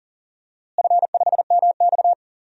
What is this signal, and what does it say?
Morse code, heard as a keyed tone, about 39 wpm, 700 Hz. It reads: F5MX
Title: DXCC-100 Call Signs - 4 Characters - Repeated 3x - 40wpm